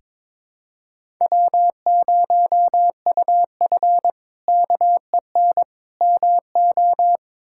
Morse code, heard as a keyed tone, about 22 wpm, 700 Hz. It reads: W0UF KEN MO
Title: K1USN SST QSO Practice - 22wpm